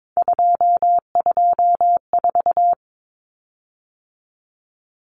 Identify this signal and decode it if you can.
Morse code, heard as a keyed tone, about 22 wpm, 700 Hz. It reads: 224
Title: Common Cut Numbers - 3-Digit 22wpm